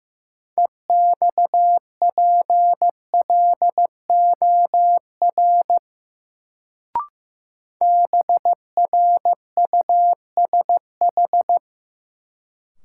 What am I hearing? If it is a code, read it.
Morse code, heard as a keyed tone, about 15 wpm, 700 Hz. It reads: EXPLOR BRUSH
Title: Sentences from Top 1100 Words - New Word Drill Alt - 15wpm